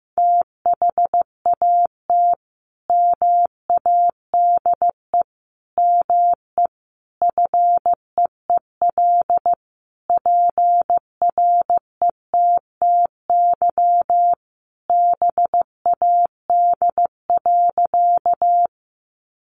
Morse code, heard as a keyed tone, about 15 wpm, 700 Hz. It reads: THAT MADE ME FEEL PRETTY BAD.
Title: Sentences from Top 1000 Words 15wpm